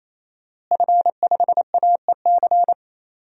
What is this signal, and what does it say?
Morse code, heard as a keyed tone, about 28 wpm, 700 Hz. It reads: F5AEC
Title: DXCC-100 Call Signs - 5 Characters - Repeated 3x - 28wpm